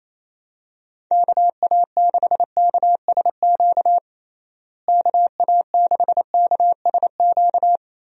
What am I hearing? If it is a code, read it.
Morse code, heard as a keyed tone, about 28 wpm, 700 Hz. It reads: KA6KSQ KA6KSQ
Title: USA Call Signs - 6 Characters - Repeated 3x - 28wpm